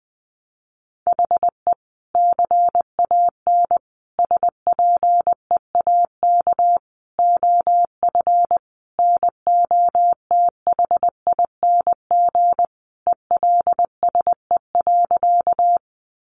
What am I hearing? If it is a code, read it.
Morse code, heard as a keyed tone, about 20 wpm, 700 Hz. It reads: HE CAN SPEAK OF NOTHING ELSE.
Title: Sentences from Top 1100 Words 20wpm